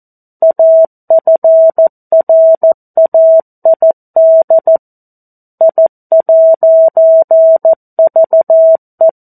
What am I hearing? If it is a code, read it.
Morse code, heard as a keyed tone, about 14 wpm, 650 Hz. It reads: AFRAID I'VE
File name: 02-Chapter-02 14 WPM